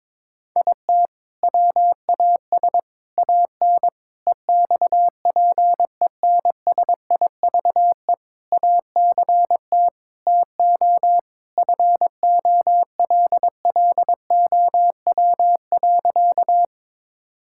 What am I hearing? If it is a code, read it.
Morse code, heard as a keyed tone, about 22 wpm, 700 Hz. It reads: IT WAS AN EXPENSIVE ACT TO FOLLOW.
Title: Sentences from Top 1000 Words - Review Part 6 - 22wpm